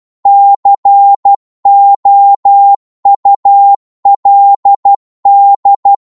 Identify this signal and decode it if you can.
Morse code, heard as a keyed tone, about 12 wpm, 800 Hz. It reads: COULD